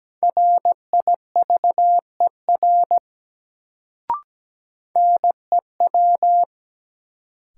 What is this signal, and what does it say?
Morse code, heard as a keyed tone, about 17 wpm, 700 Hz. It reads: RIVER NEW
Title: Sentences from Top 300 Words - New Word Drill - 17wpm